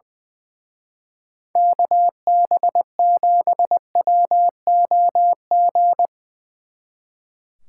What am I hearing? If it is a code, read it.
Morse code, heard as a keyed tone, about 21 wpm, 700 Hz. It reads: KB7WOG
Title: Speed Racing - USA 2x3 Call Signs 20wpm